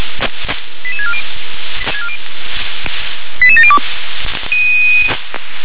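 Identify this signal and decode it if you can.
Five-tone selective call (ZVEI), heard as quick runs of five tones, tones of about 70 ms. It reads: 88640 88702
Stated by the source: ZVEI five-tone selective call received by SDR.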